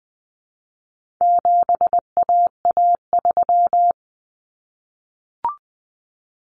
Morse code, heard as a keyed tone, about 19 wpm, 700 Hz. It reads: 7AA3
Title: Common Cut Numbers - 4-Digit 20wpm